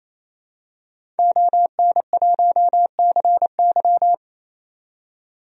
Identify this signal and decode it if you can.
Morse code, heard as a keyed tone, about 28 wpm, 700 Hz. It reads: ON1CY
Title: DXCC-100 Call Signs - 5 Characters - 28wpm